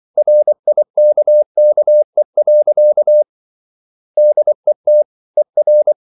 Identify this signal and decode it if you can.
Morse code, heard as a keyed tone, about 24 wpm, 600 Hz. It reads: RIKKE. DET ER
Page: Ferie 24 wpm